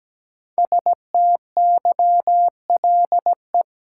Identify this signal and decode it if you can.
Morse code, heard as a keyed tone, about 17 wpm, 700 Hz. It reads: STYLE
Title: Sentences from Top 800 Words - New Word Drill - 17wpm